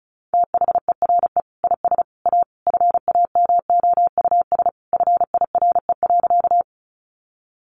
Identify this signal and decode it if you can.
Morse code, heard as a keyed tone, about 34 wpm, 700 Hz. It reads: THERE IS A FAMOUS FIRE.
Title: Sentences from Top 1500 Words 35wpm